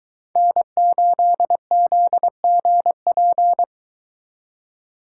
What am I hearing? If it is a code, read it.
Morse code, heard as a keyed tone, about 22 wpm, 700 Hz. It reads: N8ZGP